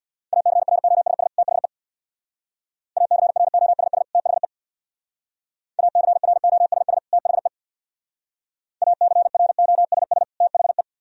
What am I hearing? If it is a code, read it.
Morse code, heard as a keyed tone, about 60 wpm, 700 Hz. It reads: ACROSS THE ACROSS THE ACROSS THE ACROSS THE
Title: Speed-Racing - Sets of 2 Words 50wpm